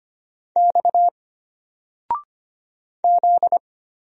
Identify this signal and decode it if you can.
Morse code, heard as a keyed tone, about 25 wpm, 700 Hz. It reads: X Z
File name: Single-Letter-Number-25wpm